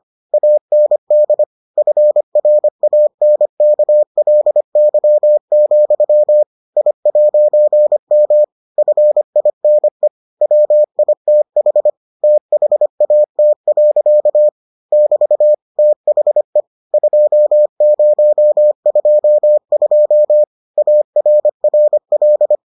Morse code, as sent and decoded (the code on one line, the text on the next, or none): .- -. -.. / ..-. .-. .- -. -.- .-.. -.-- --..-- / .. .----. -- / ..-. .. -. / .-- .. - .... / - .... .- - .-.-.- / -...- / - .... / ..--- ----- ..--- ..--- / .- .-. .-. .-..
AND FRANKLY, I'M FINE WITH THAT. <BT> THE 2022 ARRL